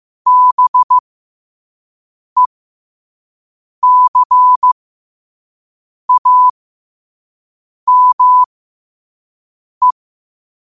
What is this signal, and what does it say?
Morse code, heard as a keyed tone, about 15 wpm, spaced as for 5 wpm, 1000 Hz. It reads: BECAME
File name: qotdfiles / 5wpm